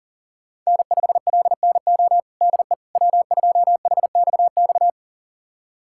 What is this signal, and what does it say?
Morse code, heard as a keyed tone, about 40 wpm, 700 Hz. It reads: N5LNO DE W2HXX
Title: USA Callsign DE Callsign - 5 Characters - 40wpm